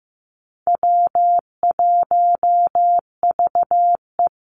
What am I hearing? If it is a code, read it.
Morse code, heard as a keyed tone, about 15 wpm, 700 Hz. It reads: W1VE
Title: Easy Call Signs - 15wpm